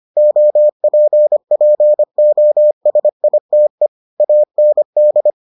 Morse code, heard as a keyed tone, about 25 wpm, 600 Hz. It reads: OPPOSITE AND